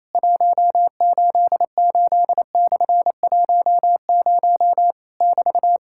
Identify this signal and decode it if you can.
Morse code, heard as a keyed tone, about 27 wpm, 700 Hz. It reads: 188/10 <BT>
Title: Worked All Europe DX Contest - QTC Exchange - 28wpm